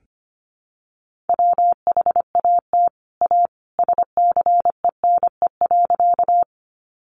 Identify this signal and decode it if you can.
Morse code, heard as a keyed tone, about 25 wpm, 700 Hz. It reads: WHAT A SCENE.
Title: Sentences from Top 1100 Words 25wpm